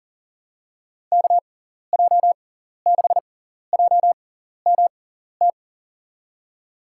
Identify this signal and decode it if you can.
Morse code, heard as a keyed tone, about 40 wpm, spaced as for 14 wpm, 700 Hz. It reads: KJ6JMT
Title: USA Call Signs - 6 Characters - 6x Character Spacing - 40wpm